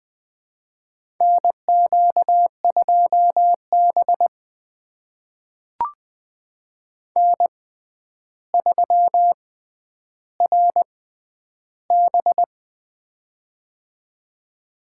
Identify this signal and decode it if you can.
Morse code, heard as a keyed tone, about 20 wpm, spaced as for 10 wpm, 700 Hz. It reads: NQ2B N3RB
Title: USA Call Signs - 4 Characters - 6x Character Spacing - 20wpm